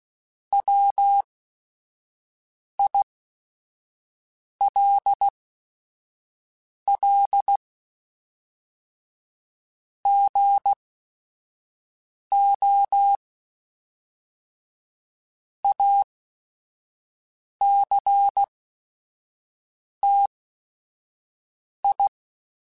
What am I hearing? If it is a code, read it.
Morse code, heard as a keyed tone, about 16 wpm, spaced as for 5 wpm, 800 Hz. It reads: WILL GO ACTI